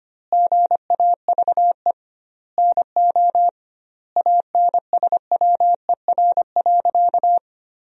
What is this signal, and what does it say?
Morse code, heard as a keyed tone, about 25 wpm, 700 Hz. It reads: GAVE NO ANSWER.
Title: 2x Word-Spacing - Sentences from Top 500 Words - 25wpm